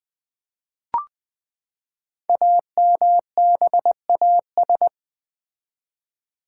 Morse code, heard as a keyed tone, about 20 wpm, 700 Hz. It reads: AMBAS
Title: Build A Word - 5 Letters At A Time - 10-Letter Words - 20wpm